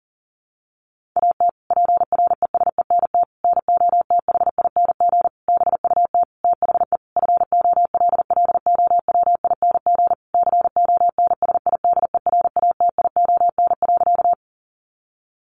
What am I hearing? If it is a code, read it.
Morse code, heard as a keyed tone, about 40 wpm, 700 Hz. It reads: AT PRESENT NOTHING BUT THE FOLLOWING CONSIDERATION.
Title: Sentences from Top 1500 Words 40wpm